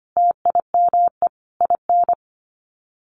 Morse code, heard as a keyed tone, about 25 wpm, 700 Hz. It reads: TIME IN
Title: Sets of 5 Words 25wpm